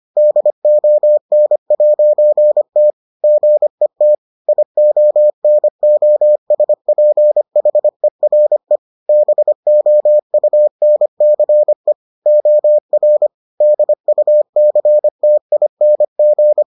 Morse code, heard as a keyed tone, about 25 wpm, 600 Hz. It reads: DON'T GET IONOSPHERE BOUNCE OR DUCTING